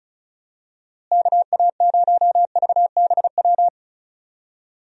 Morse code, heard as a keyed tone, about 34 wpm, 700 Hz. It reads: KA0VBW
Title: USA Call Signs - 6 Characters - Repeated 2x - 35wpm